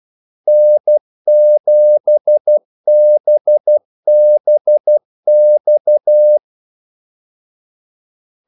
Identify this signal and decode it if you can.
Morse code, heard as a keyed tone, about 12 wpm, 600 Hz. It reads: N7BBX
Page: 2 ekstra ordmellemrum mellem kaldesignalerne